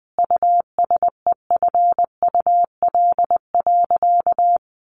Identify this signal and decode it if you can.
Morse code, heard as a keyed tone, about 20 wpm, 700 Hz. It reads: USEFUL.
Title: Sentences from Top 1400 Words 20wpm